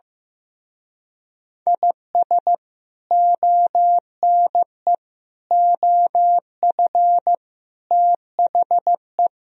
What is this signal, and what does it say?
Morse code, heard as a keyed tone, about 16 wpm, 700 Hz. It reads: IS ONE OF THE
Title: Speed-Racing - Sets of 5 Words 15wpm